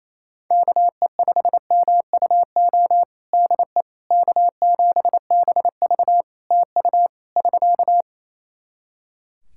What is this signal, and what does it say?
Morse code, heard as a keyed tone, about 27 wpm, 700 Hz. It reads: KE5MUO DE K7BV TU <SK>
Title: K1USN SST QSO Practice 28wpm